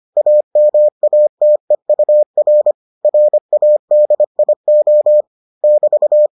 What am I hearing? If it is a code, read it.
Morse code, heard as a keyed tone, about 25 wpm, 600 Hz. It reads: AMATEUR RADIO <BT>